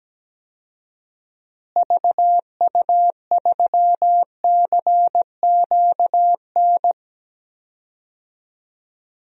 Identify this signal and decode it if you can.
Morse code, heard as a keyed tone, about 17 wpm, 700 Hz. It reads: VU3CQN
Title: DXCC-100 Call Signs - 6 Characters - Repeated 3x - 17wpm